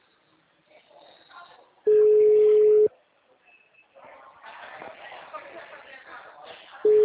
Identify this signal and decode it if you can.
Ringback tone (Europe, 425 Hz), with speech.